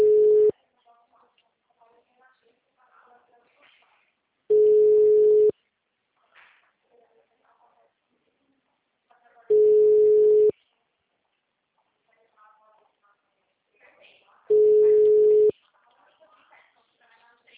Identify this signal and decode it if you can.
Ringback tone (Europe, 425 Hz), with speech.
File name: out- -2029- - - .33219